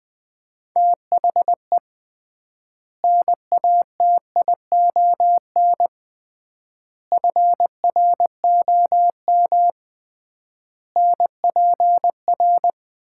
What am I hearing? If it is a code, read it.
Morse code, heard as a keyed tone, about 21 wpm, 700 Hz. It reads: THE NATION FROM NPR
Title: 3x Word-Spacing - Sets of 5 Words - 20wpm